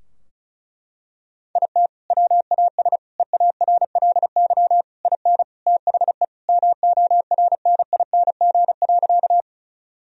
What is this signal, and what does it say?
Morse code, heard as a keyed tone, about 36 wpm, 700 Hz. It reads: IT WAS EARLY IN THE MORNING.
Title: Sentences from Top 1000 Words - Review Part 2 - 35wpm